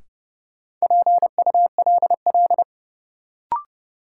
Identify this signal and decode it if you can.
Morse code, heard as a keyed tone, about 30 wpm, 700 Hz. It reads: PULL
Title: Sentences from Top 400 Words - New Word Drill - 30wpm